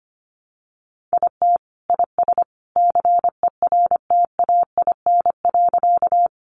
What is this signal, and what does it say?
Morse code, heard as a keyed tone, about 25 wpm, 700 Hz. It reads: IT IS CERTAIN.
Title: Sentences from Top 500 Words 25wpm